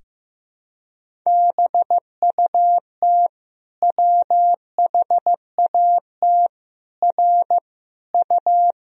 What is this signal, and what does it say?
Morse code, heard as a keyed tone, about 15 wpm, 700 Hz. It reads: BUT WHAT R U
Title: Sentences from Top 100 Words and Abbreviations 15wpm